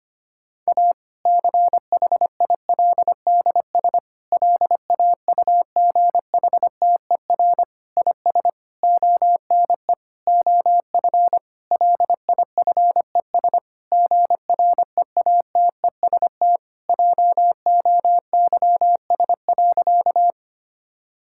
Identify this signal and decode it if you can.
Morse code, heard as a keyed tone, about 25 wpm, 700 Hz. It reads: A CHILDS LAUGHTER IS ONE OF LIFES GREATEST JOYS.
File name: practice / 25wpm / Difficult Sentences with Hints - 25wpm